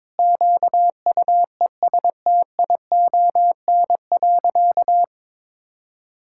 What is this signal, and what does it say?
Morse code, heard as a keyed tone, about 22 wpm, 700 Hz. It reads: QUESTION.
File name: practice / 22wpm / Sentences with 7-Word Phrases - 22wpm